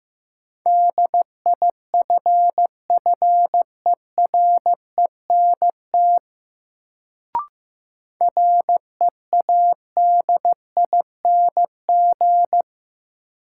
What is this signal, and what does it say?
Morse code, heard as a keyed tone, about 15 wpm, 700 Hz. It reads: DIFFERENT READING
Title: Sentences from Top 600 Words - New Word Drill - 15wpm